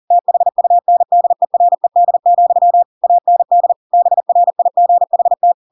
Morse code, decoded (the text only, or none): THUNDERED, AND BRIGHT